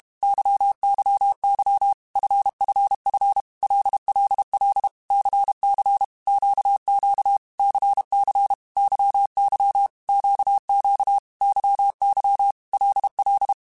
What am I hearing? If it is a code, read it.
Morse code, heard as a keyed tone, about 32 wpm, 800 Hz. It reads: YYY FFF LLL CC QQ CC YY QQ YY LL
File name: Groups@30 / Groups@30 mp3s / LFCYQ@30